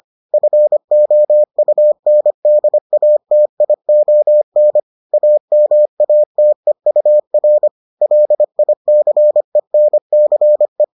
Morse code, heard as a keyed tone, about 25 wpm, 600 Hz. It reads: FOUNDATION AMATEUR LICENCE